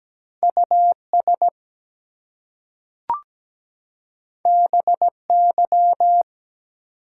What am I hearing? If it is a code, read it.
Morse code, heard as a keyed tone, about 17 wpm, 700 Hz. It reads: US BY
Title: Two Letter Words - 17wpm